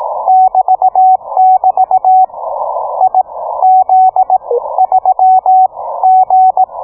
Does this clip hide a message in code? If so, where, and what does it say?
Yes: Morse code from 0.3 to 6.6 s, heard as a keyed tone, about 18 wpm, spaced as for 14 wpm, 750 Hz, with radio noise. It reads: <BT><BT> IZ3G